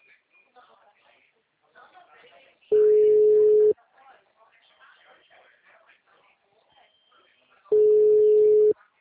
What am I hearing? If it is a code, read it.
Ringback tone (Europe, 425 Hz), with speech.
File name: out- -2004- - - .41548